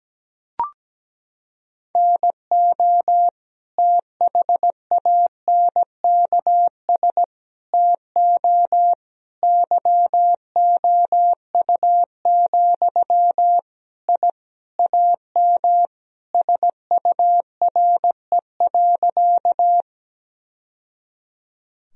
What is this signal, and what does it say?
Morse code, heard as a keyed tone, about 17 wpm, 700 Hz. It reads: NO THANKS TO YOU, I AM SURE.
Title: Sentences from Top 1000 Words - Review Part 3 - 17wpm